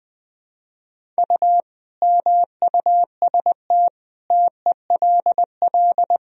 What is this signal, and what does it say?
Morse code, heard as a keyed tone, about 20 wpm, 700 Hz. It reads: U MUST TELL
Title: Sentences from Top 200 Words and Abbreviations 20wpm